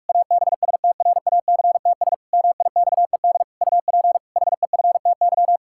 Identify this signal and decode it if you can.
Morse code, heard as a keyed tone, about 46 wpm, 700 Hz. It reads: ABSTRACTS MIXED UP HEFTY
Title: Pangram Challenge v2 - 45wpm